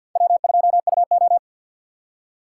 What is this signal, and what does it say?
Morse code, heard as a keyed tone, about 51 wpm, 700 Hz. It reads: W2UO
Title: USA Call Signs - 4 Characters - 2x Character Spacing - 50wpm